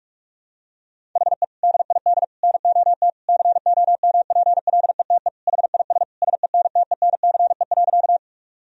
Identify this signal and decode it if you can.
Morse code, heard as a keyed tone, about 46 wpm, 700 Hz. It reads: HE DID NOT COMPLETE HIS SENTENCE.
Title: Sentences from Top 1000 Words - Review Part 1 - 45wpm